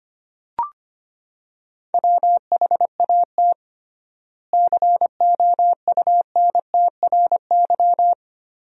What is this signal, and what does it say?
Morse code, heard as a keyed tone, about 25 wpm, 700 Hz. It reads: WHAT COUNTRY
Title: 3x Word-Spacing - Sentences from Top 300 Words - 25wpm